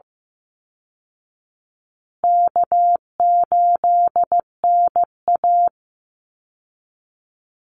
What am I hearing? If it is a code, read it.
Morse code, heard as a keyed tone, about 15 wpm, 700 Hz. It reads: K8NA
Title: Easy Call Signs Repeated 3x - 15wpm